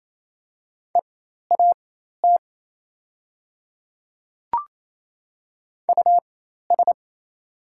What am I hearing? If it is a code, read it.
Morse code, heard as a keyed tone, about 27 wpm, spaced as for 13 wpm, 700 Hz. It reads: EAT US
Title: 4x Character Spacing - Top 300 Words - 28wpm